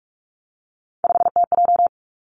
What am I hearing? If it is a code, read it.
Morse code, heard as a keyed tone, about 46 wpm, 700 Hz. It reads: 5TJ
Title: Three Challenging Letter-Number Combinations 45wpm